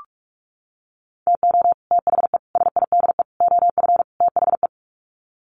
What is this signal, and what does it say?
Morse code, heard as a keyed tone, about 44 wpm, 700 Hz. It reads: TO THE SIDE OF THE
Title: Sets of 5 Words 45wpm